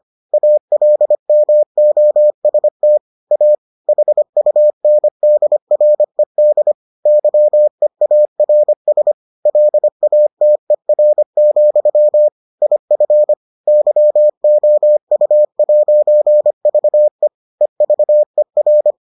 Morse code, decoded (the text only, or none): ALMOST A HUNDRED YEARS LATER, IF YOU'VE EVER